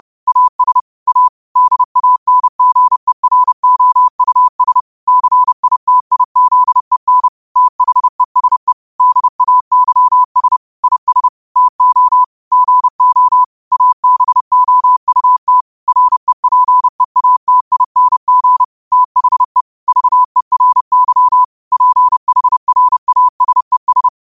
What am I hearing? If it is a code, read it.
Morse code, heard as a keyed tone, about 30 wpm, 1000 Hz. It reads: AS A DANGEROUS CITIZEN THESE DAYS IS TO GO ABOUT REPEATING THE VERY PHRASES